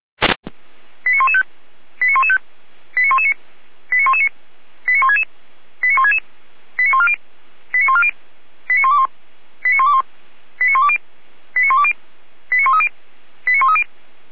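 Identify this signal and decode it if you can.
Five-tone selective call (ZVEI), heard as quick runs of five tones, tones of about 70 ms. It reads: 89115 89115 89119 89119 89166 89166 89140 89140 89121 89121 89120 89120 89130 89130
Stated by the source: ZVEI five-tone selective call received on an SDR.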